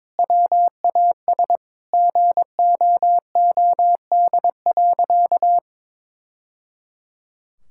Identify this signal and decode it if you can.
Morse code, heard as a keyed tone, about 22 wpm, 700 Hz. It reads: WAS GOOD.
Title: Sentences from Top 1000 Words - Review Part 5 - 22wpm